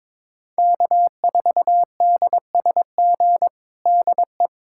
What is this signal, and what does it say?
Morse code, heard as a keyed tone, about 22 wpm, 700 Hz. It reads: K4DSG DE
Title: USA Callsign DE Callsign - 5 Characters - 22wpm